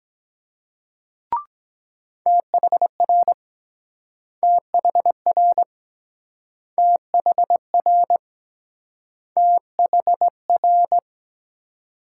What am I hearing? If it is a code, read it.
Morse code, heard as a keyed tone, about 22 wpm, 700 Hz. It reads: THR THR THR THR